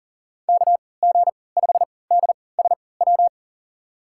Morse code, heard as a keyed tone, about 40 wpm, spaced as for 23 wpm, 700 Hz. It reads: KG5DSW
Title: USA Call Signs - 6 Characters - 3x Character Spacing - 40wpm